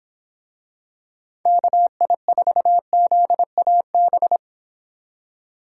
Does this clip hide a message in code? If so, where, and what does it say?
Yes: Morse code from 1.5 to 4.4 s, heard as a keyed tone, about 25 wpm, 700 Hz. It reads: KI4ZAB